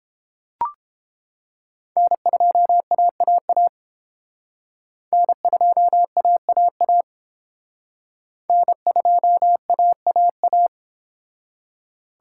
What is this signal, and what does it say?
Morse code, heard as a keyed tone, about 29 wpm, 700 Hz. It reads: N2AAA N2AAA N2AAA